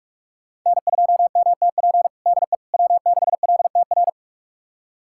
Morse code, heard as a keyed tone, about 44 wpm, 700 Hz. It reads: N2MTP DE W6LTR
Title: USA Callsign DE Callsign - 5 Characters - 45wpm